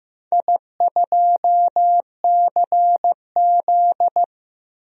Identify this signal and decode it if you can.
Morse code, heard as a keyed tone, about 15 wpm, 700 Hz. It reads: I2CZ
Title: DXCC-100 Call Signs - 4 Characters - 15wpm